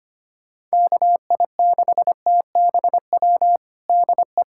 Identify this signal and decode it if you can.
Morse code, heard as a keyed tone, about 25 wpm, 700 Hz. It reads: KI6TBW DE
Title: USA Callsign DE Callsign - 6 Characters - 25wpm